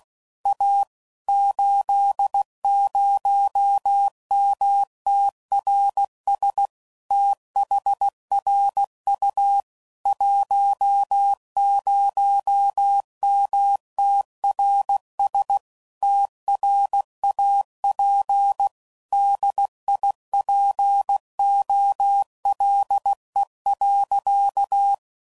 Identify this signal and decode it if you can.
Morse code, heard as a keyed tone, about 16 wpm, 800 Hz. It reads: A 80MTRS THRU 10MTRS TRAP DIPOLE.
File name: CW Code Groups @15WPM / QSOs / WD0FBT Spaced